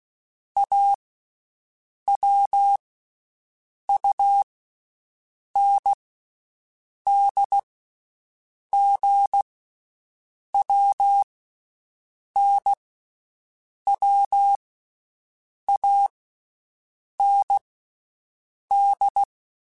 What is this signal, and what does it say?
Morse code, heard as a keyed tone, about 16 wpm, 800 Hz. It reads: A W U N D G W N W A N D
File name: Basic Letter Groups By7Groups / AWUNDG / AWUNDG@10